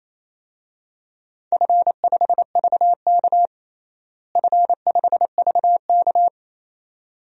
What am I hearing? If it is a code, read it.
Morse code, heard as a keyed tone, about 28 wpm, 700 Hz. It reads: F5VK F5VK
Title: DXCC-100 Call Signs - 4 Characters - Repeated 2x - 28wpm